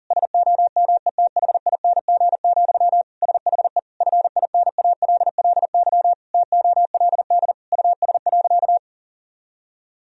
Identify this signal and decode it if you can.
Morse code, heard as a keyed tone, about 40 wpm, 700 Hz. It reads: SOMETHING, SHE FINALLY TOLD US.
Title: Sentences from Top 1000 Words - Review Part 3 - 40wpm